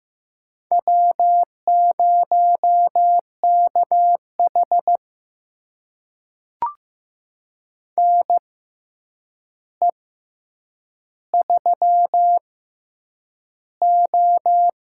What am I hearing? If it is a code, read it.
Morse code, heard as a keyed tone, about 15 wpm, spaced as for 8 wpm, 700 Hz. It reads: W0KH NE3O
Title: USA Call Signs - 6x Character Spacing - 15wpm